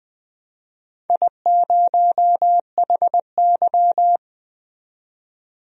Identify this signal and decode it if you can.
Morse code, heard as a keyed tone, about 20 wpm, 700 Hz. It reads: I0HY
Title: DXCC-100 Call Signs - 4 Characters - 20wpm